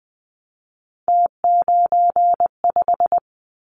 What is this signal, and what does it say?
Morse code, heard as a keyed tone, about 20 wpm, 700 Hz. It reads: T95
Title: Callsign Prefixes for Top 100 DXCC - 20wpm